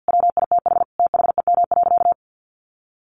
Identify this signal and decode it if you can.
Morse code, heard as a keyed tone, about 50 wpm, 700 Hz. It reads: WITH THEM.
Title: Sentences from Top 800 Words 50wpm